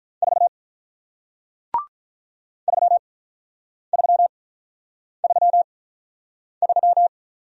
Morse code, heard as a keyed tone, about 47 wpm, spaced as for 9 wpm, 700 Hz. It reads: 4 3333